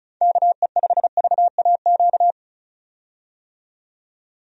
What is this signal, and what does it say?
Morse code, heard as a keyed tone, about 36 wpm, 700 Hz. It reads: KE5VAQ
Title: USA Call Signs - 6 Characters - 35wpm